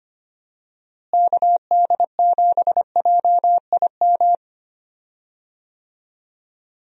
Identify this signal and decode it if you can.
Morse code, heard as a keyed tone, about 25 wpm, 700 Hz. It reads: KD7JIM
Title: USA Call Signs - 6 Characters - Repeated 3x - 25wpm